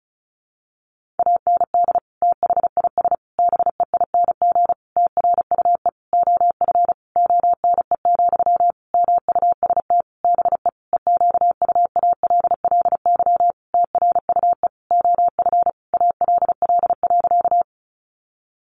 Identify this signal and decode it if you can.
Morse code, heard as a keyed tone, about 34 wpm, 700 Hz. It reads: AND THIS BEING TRUE OF ONE, MUST BE EQUALLY TRUE OF ALL.
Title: Sentences from Top 1400 Words 35wpm